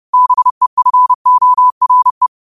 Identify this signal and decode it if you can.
Morse code, heard as a keyed tone, about 30 wpm, 1000 Hz. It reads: BEFORE
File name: qotdfiles / 30wpm